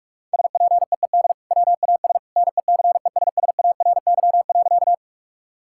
Morse code, heard as a keyed tone, about 44 wpm, 700 Hz. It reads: SPEED WAS NECESSARY.
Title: Sentences from Top 1000 Words - Review Part 5 - 45wpm